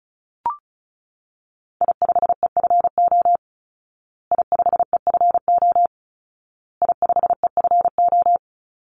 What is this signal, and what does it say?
Morse code, heard as a keyed tone, about 34 wpm, 700 Hz. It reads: I5EFO I5EFO I5EFO
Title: Call Signs Repeated 2x - 35wpm